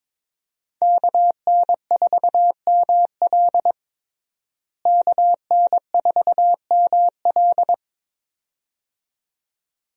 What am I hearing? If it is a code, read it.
Morse code, heard as a keyed tone, about 22 wpm, 700 Hz. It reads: KN4ML KN4ML
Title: USA Call Signs - 5 Characters - Repeated 2x - 22wpm